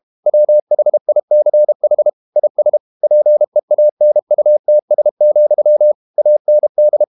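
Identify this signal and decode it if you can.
Morse code, heard as a keyed tone, about 32 wpm, 600 Hz. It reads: WHICH IS PEANUTS, AND
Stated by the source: Morse code practice at 32 wpm from CW med Gnister, reading Ken Follett's novel Never in English.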